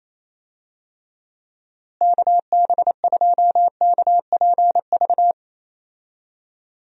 Morse code, heard as a keyed tone, about 27 wpm, 700 Hz. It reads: KB2KPV
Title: USA Call Signs - 6 Characters - Repeated 3x - 28wpm